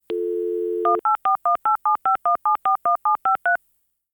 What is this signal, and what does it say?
DTMF, heard as telephone touch tones, tones of about 120 ms. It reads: 18418751741753
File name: modem-dialing-01